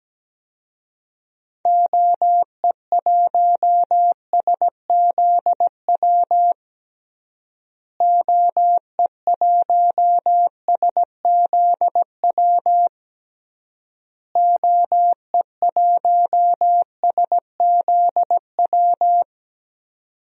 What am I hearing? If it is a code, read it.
Morse code, heard as a keyed tone, about 17 wpm, 700 Hz. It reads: OE1SZW OE1SZW OE1SZW
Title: DXCC-100 Call Signs - 6 Characters - Repeated 3x - 17wpm